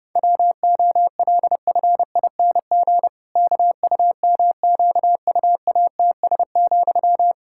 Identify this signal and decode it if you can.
Morse code, heard as a keyed tone, about 30 wpm, 700 Hz. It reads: WOLFING KUMQUATS,
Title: Pangram Challenge v2 - 30wpm